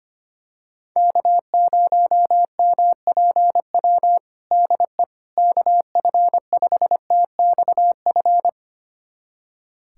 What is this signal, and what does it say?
Morse code, heard as a keyed tone, about 25 wpm, 700 Hz. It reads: K0MPW DE KF5TXF
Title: USA Callsign DE Callsign 25wpm